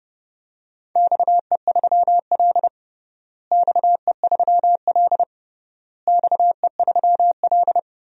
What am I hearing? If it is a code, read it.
Morse code, heard as a keyed tone, about 32 wpm, 700 Hz. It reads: XE3L XE3L XE3L
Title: DXCC-100 Call Signs - 4 Characters - Repeated 3x - 30wpm